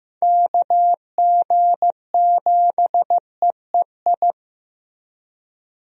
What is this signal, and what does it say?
Morse code, heard as a keyed tone, about 15 wpm, 700 Hz. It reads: KG7EEI
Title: USA Call Signs - 6 Characters - Repeated 2x - 15wpm